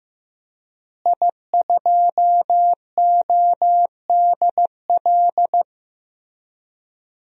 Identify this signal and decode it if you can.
Morse code, heard as a keyed tone, about 15 wpm, 700 Hz. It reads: I2ODL
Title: DXCC-100 Call Signs - 5 Characters - Repeated 3x - 15wpm